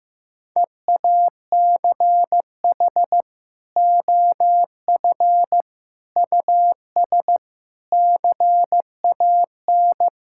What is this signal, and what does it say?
Morse code, heard as a keyed tone, about 15 wpm, 700 Hz. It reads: EACH OF US CAN
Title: Sentences from Top 1000 Words - Review Part 5 - 15wpm